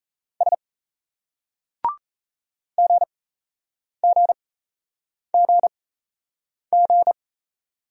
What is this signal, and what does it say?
Morse code, heard as a keyed tone, about 36 wpm, spaced as for 8 wpm, 700 Hz. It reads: S GGGG